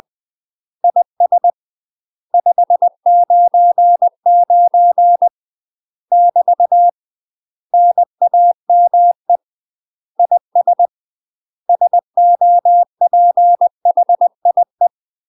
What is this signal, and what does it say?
Morse code, heard as a keyed tone, about 20 wpm, 700 Hz. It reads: IS 599 <BT> NAME IS SOPHIE
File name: MASTER QSOs TRAINING/20WPM